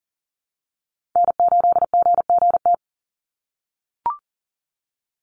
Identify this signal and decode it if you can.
Morse code, heard as a keyed tone, about 40 wpm, 700 Hz. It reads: N8GGT
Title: USA Call Signs 40wpm